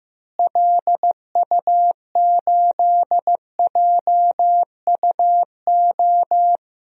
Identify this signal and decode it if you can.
Morse code, heard as a keyed tone, about 15 wpm, 700 Hz. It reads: LU8JUO
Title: DXCC-100 Call Signs - 6 Characters - Repeated 3x - 15wpm